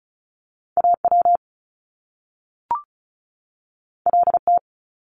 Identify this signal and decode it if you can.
Morse code, heard as a keyed tone, about 36 wpm, 700 Hz. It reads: AW LT
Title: Top Two Letter Combinations in Words 35wpm